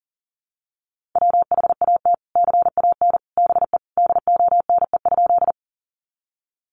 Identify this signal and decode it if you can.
Morse code, heard as a keyed tone, about 40 wpm, 700 Hz. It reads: WHAT CAN BE DONE?
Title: Sentences from Top 500 Words 40wpm